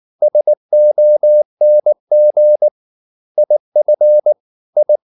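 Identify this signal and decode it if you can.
Morse code, heard as a keyed tone, about 19 wpm, 600 Hz. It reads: SONG IF I
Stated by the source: Morse code at 19 wpm from CW med Gnister, reading Ken Follett's novel Never in English.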